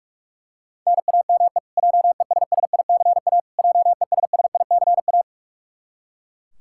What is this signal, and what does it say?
Morse code, heard as a keyed tone, about 41 wpm, 700 Hz. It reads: NAME JESSICA JESSICA
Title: Practice QSO - Exchange 1 - Beginning - 45wpm